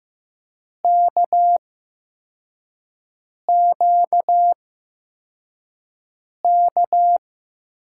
Morse code, heard as a keyed tone, about 15 wpm, spaced as for 4 wpm, 700 Hz. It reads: KQK
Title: USA Call Signs - 8x Character Spacing - 15wpm